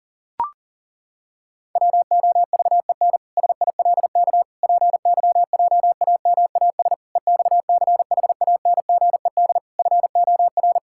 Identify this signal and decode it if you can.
Morse code, heard as a keyed tone, about 37 wpm, 700 Hz. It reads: WOVEN SILK PYJAMAS EXCHANGED FOR
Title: Pangram Challenge v2 - 40wpm